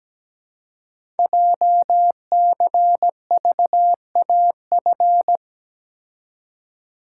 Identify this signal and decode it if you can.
Morse code, heard as a keyed tone, about 17 wpm, 700 Hz. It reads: JCVAF
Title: Code Groups - 5 Letters - 17wpm